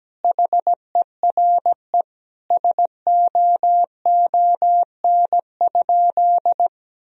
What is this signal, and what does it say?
Morse code, heard as a keyed tone, about 17 wpm, 700 Hz. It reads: HERE SOON?
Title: Sentences from Top 1000 Words - Review Part 3 - 17wpm